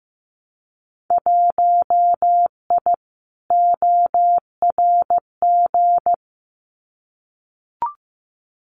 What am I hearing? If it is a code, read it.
Morse code, heard as a keyed tone, about 15 wpm, 700 Hz. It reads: 1I ORG
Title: Winter Field Day Exchanges 15wpm